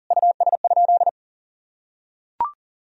Morse code, heard as a keyed tone, about 40 wpm, 700 Hz. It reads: US?
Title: Sentences from Top 1000 Words - Review Part 3 - 40wpm